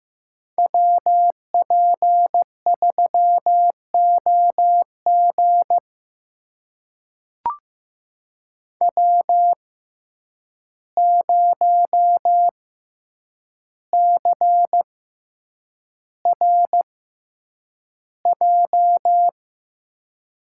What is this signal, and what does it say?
Morse code, heard as a keyed tone, about 15 wpm, spaced as for 8 wpm, 700 Hz. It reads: WP3OG W0CRJ
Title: USA Call Signs - 5 Characters - 6x Character Spacing - 15wpm